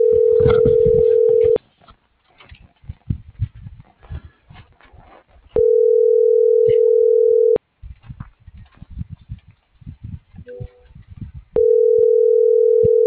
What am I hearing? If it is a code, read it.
Ringback tone (North America), 440 + 480 Hz.